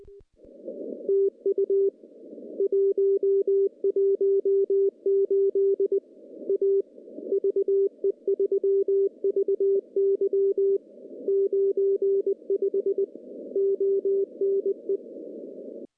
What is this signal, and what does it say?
Morse code, heard as a keyed tone, about 20 wpm, 390 Hz. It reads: TU 118 A VE3VY 95 ONE
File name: contests / CW-SS-k5tr- -0400